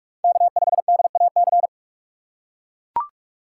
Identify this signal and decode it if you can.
Morse code, heard as a keyed tone, about 44 wpm, 700 Hz. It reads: K5DAC
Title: USA Call Signs - 5 Characters - 6x Character Spacing - 45wpm